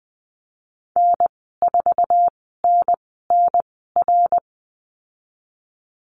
Morse code, heard as a keyed tone, about 20 wpm, spaced as for 15 wpm, 700 Hz. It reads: N4NNR
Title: 088 Lesson 15g - Number 5 - Call Signs 20wpm